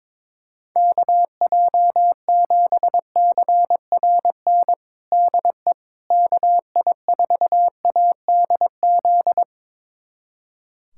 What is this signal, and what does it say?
Morse code, heard as a keyed tone, about 22 wpm, 700 Hz. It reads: KJ7CRN DE KI4ADZ
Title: USA Callsign DE Callsign - 6 Characters - 22wpm